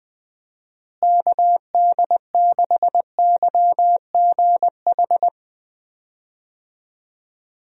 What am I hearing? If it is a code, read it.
Morse code, heard as a keyed tone, about 20 wpm, 700 Hz. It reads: KD6YGH
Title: USA Call Signs - 6 Characters - Repeated 2x - 20wpm